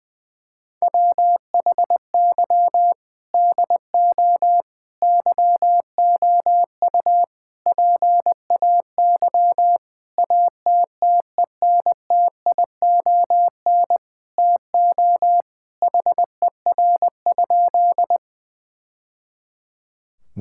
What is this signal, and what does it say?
Morse code, heard as a keyed tone, about 20 wpm, 700 Hz. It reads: WHY DO YOU PAY ATTENTION TO HER?
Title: Sentences from Top 1000 Words - Review Part 6 - 20wpm